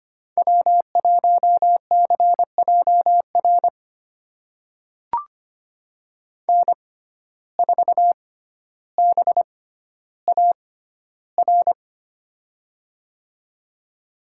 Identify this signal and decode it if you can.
Morse code, heard as a keyed tone, about 25 wpm, spaced as for 13 wpm, 700 Hz. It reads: W1CJR N4BAR
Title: USA Call Signs - 6x Character Spacing - 25wpm